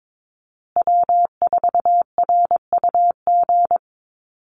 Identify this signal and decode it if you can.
Morse code, heard as a keyed tone, about 22 wpm, 700 Hz. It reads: W4RUG
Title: USA Call Signs Repeated 2x 22wpm